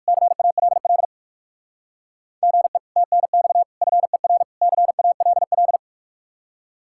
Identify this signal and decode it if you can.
Morse code, heard as a keyed tone, about 59 wpm, 700 Hz. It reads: CALL GE TNX FER CALL